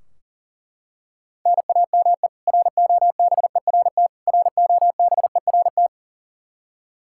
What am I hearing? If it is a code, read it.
Morse code, heard as a keyed tone, about 40 wpm, 700 Hz. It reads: NAME ROBERT ROBERT
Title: Practice QSO - Exchange 1 - Beginning - 40wpm